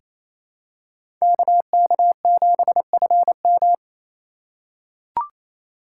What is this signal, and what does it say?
Morse code, heard as a keyed tone, about 27 wpm, 700 Hz. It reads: KK7FM
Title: USA Call Signs - 5 Characters - 4x Character Spacing - 28wpm